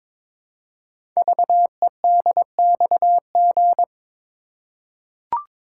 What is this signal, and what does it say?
Morse code, heard as a keyed tone, about 22 wpm, 700 Hz. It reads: VEDXG
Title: Code Groups - 5 Letters - 22wpm